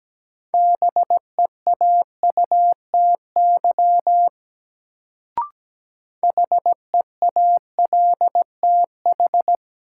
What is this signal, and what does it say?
Morse code, heard as a keyed tone, about 17 wpm, 700 Hz. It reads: BEAUTY HEALTH
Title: Sentences from Top 600 Words - New Word Drill - 17wpm